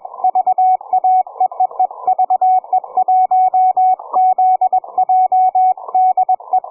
Morse code, heard as a keyed tone, about 23 wpm, 760 Hz, with radio noise. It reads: VA E E E VE1ZJ DE
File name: cw forum / ve1zj 1830 160m 04h00